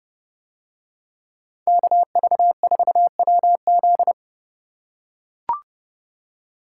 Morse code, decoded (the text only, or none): KV4WZ